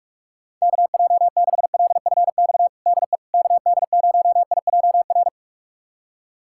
Morse code, heard as a keyed tone, about 42 wpm, 700 Hz. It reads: KJ6LFX DE KD0IJR
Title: USA Callsign DE Callsign - 6 Characters - 45wpm